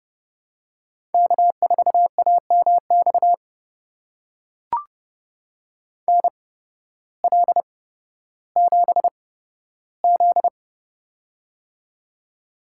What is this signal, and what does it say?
Morse code, heard as a keyed tone, about 30 wpm, spaced as for 14 wpm, 700 Hz. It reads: K4AMX NL7Z
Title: USA Call Signs - 8x Character Spacing - 30wpm